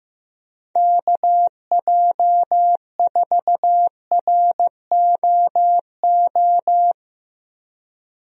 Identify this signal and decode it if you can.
Morse code, heard as a keyed tone, about 15 wpm, 700 Hz. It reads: KJ4ROO
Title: USA Call Signs - 6 Characters - 15wpm